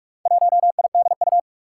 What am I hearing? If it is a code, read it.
Morse code, heard as a keyed tone, about 44 wpm, 700 Hz. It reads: 1IDU
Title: Four Challenging Letter-Number Combinations 45wpm